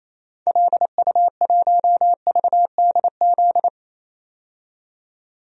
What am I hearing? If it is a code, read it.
Morse code, heard as a keyed tone, about 28 wpm, 700 Hz. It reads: LU1VDZ
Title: DXCC-100 Call Signs - 6 Characters - 28wpm